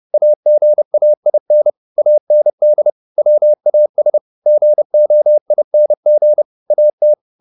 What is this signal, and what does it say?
Morse code, heard as a keyed tone, about 30 wpm, 600 Hz. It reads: AGAIN AND WAS GOING AT
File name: the old man and the sea / the old man and the sea 1 30wpm